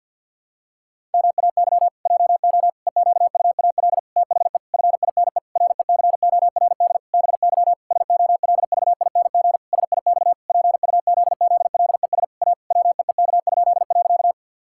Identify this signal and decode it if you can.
Morse code, heard as a keyed tone, about 50 wpm, 700 Hz. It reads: MAY JO EQUAL THE FINE RECORD BY SOLVING SIX PUZZLES A WEEK?.
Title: Pangram Challenge v2 - 50wpm